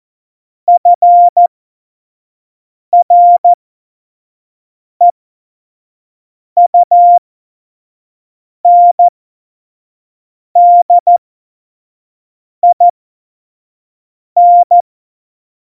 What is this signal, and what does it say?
Morse code, heard as a keyed tone, about 14 wpm, spaced as for 5 wpm, 700 Hz. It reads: FREUNDIN
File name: Morsekurs / Witze / Witz 06. November 2018, 25 F70